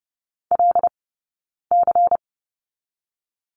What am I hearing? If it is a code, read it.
Morse code, heard as a keyed tone, about 30 wpm, 700 Hz. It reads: L C
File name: 092 Lesson 16d - Letter c - Two Characters 30wpm